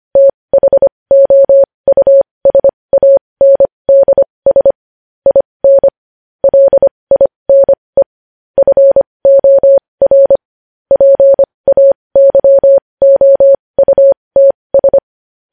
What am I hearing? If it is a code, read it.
Morse code, heard as a keyed tone, about 25 wpm, spaced as for 20 wpm, 550 Hz. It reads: THOUSANDS IN LINE FOR PAYOUTS
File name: podcast / News headlines-1x-25x20